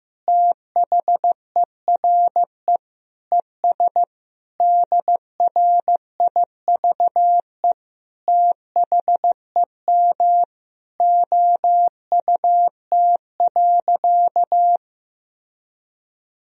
Morse code, heard as a keyed tone, about 15 wpm, 700 Hz. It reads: THERE ES DRIVE THEM OUT.
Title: Sentences from Top 500 Words and Abbreviations 15wpm